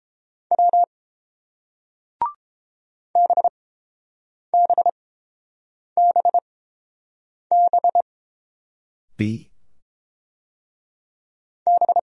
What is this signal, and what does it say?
Morse code, heard as a keyed tone, about 30 wpm, spaced as for 5 wpm, 700 Hz. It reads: W BBBBB